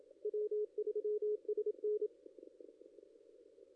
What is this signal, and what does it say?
Morse code, heard as a keyed tone, about 27 wpm, 420 Hz. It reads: W3SN
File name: contests / ss-cw-k5tr-op-n5rz -1600